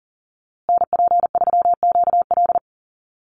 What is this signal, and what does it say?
Morse code, heard as a keyed tone, about 40 wpm, 700 Hz. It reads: NP3QL
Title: USA Call Signs Repeated 3x 40wpm